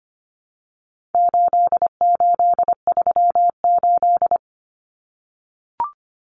Morse code, heard as a keyed tone, about 25 wpm, 700 Hz. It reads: 8838
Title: Numbers - 4-Digit 25wpm